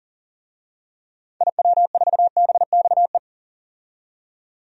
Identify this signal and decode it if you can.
Morse code, heard as a keyed tone, about 40 wpm, 700 Hz. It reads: IW4BXE
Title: DXCC-100 Call Signs - 6 Characters - 40wpm